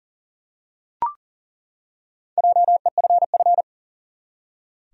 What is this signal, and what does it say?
Morse code, heard as a keyed tone, about 40 wpm, 700 Hz. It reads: JEFF
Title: USA Common Names - Max 4 Letters - 40wpm